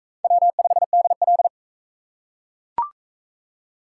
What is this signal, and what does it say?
Morse code, heard as a keyed tone, about 40 wpm, 700 Hz. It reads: W5DL